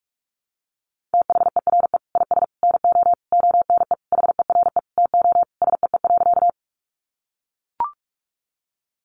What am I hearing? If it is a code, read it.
Morse code, heard as a keyed tone, about 44 wpm, 700 Hz. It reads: THERE IS NO ONE HERE TO SEE.
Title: Sentences from Top 200 Words 45wpm